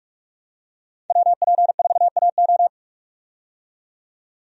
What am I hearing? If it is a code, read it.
Morse code, heard as a keyed tone, about 43 wpm, 700 Hz. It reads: WP4AO
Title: USA Call Signs - 5 Characters - 45wpm